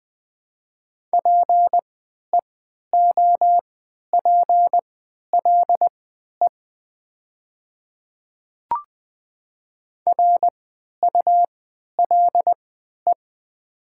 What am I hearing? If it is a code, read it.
Morse code, heard as a keyed tone, about 20 wpm, spaced as for 11 wpm, 700 Hz. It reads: PEOPLE RULE
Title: 3x Character Spacing - Top 500 Words - 20wpm